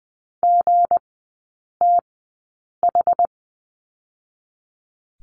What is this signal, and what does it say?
Morse code, heard as a keyed tone, about 20 wpm, spaced as for 8 wpm, 700 Hz. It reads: GTH
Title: 159 Lesson 26e - Letter g - Three Characters 20wpm